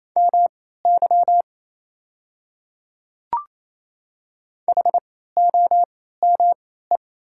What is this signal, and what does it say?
Morse code, heard as a keyed tone, about 28 wpm, spaced as for 16 wpm, 700 Hz. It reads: MY HOME
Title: 3x Character Spacing - Top 300 Words - 28wpm